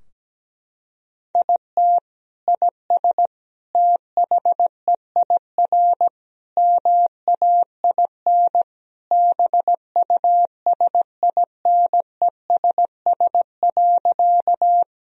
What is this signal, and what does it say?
Morse code, heard as a keyed tone, about 17 wpm, 700 Hz. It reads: IT IS THEIR MAIN BUSINESS.
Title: Sentences from Top 1000 Words - Review Part 2 - 17wpm